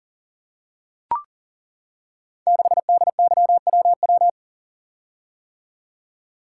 Morse code, heard as a keyed tone, about 41 wpm, 700 Hz. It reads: 6DYWW
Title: Code Groups - 5 Letter:Numbers - 40wpm